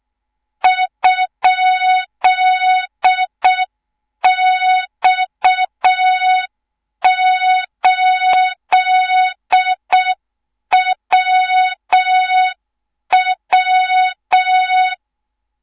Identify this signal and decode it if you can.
Morse code, heard as a keyed tone, about 6 wpm, 3700 Hz. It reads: ?X8WW
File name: Morze-HA2MN / hang / 24 GYAKORLAT